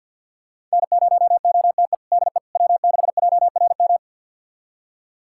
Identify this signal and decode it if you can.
Morse code, heard as a keyed tone, about 52 wpm, 700 Hz. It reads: N0OTE DE W6JRM
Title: USA Callsign DE Callsign - 5 Characters - 50wpm